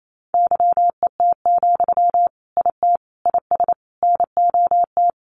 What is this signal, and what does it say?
Morse code, heard as a keyed tone, about 28 wpm, 700 Hz. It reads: YET, IT IS NOT
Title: Sentences from Top 1400 Words 28wpm